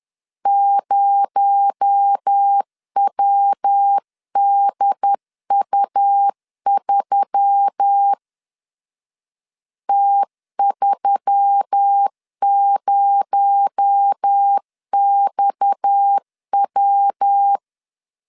Morse code, heard as a keyed tone, about 10 wpm, 800 Hz. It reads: 0WDU3 T30XW
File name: LEZIONI / 6 PKX0 / 6 PKX0 10wpm 2707